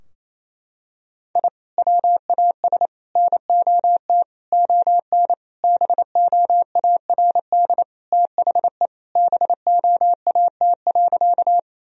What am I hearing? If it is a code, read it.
Morse code, heard as a keyed tone, about 28 wpm, 700 Hz. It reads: I WAS NOT ON BOARD THE BOAT.
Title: Sentences from Top 1000 Words - Review Part 4 - 28wpm